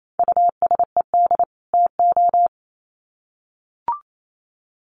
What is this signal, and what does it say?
Morse code, heard as a keyed tone, about 28 wpm, 700 Hz. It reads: USED TO
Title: Sets of 2 Words 28wpm